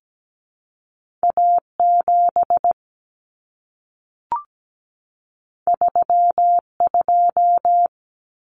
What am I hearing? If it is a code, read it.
Morse code, heard as a keyed tone, about 17 wpm, 700 Hz. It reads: A7 32
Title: Common Cut Numbers - 2-Digit 17wpm